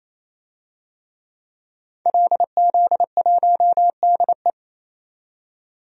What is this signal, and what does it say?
Morse code, heard as a keyed tone, about 27 wpm, 700 Hz. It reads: LZ1DE
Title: DXCC-100 Call Signs - 5 Characters - 28wpm